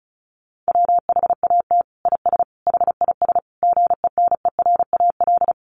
Morse code, heard as a keyed tone, about 34 wpm, 700 Hz. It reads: WHAT IS HIS GENERAL
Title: Sentences from Top 1000 Words 35wpm